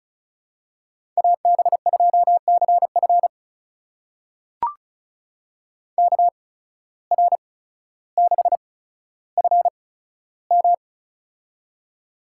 Morse code, heard as a keyed tone, about 36 wpm, spaced as for 15 wpm, 700 Hz. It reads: AB2CF KR6FM
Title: USA Call Signs - 5 Characters - 8x Character Spacing - 35wpm